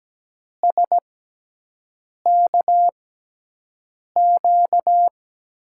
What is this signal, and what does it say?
Morse code, heard as a keyed tone, about 17 wpm, spaced as for 6 wpm, 700 Hz. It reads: SKQ
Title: USA Call Signs - 6x Character Spacing - 17wpm